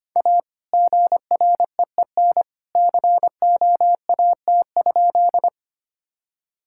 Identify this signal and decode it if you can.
Morse code, heard as a keyed tone, about 25 wpm, 700 Hz. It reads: A GREEN COAT?
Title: Sentences from Top 1600 Words 25wpm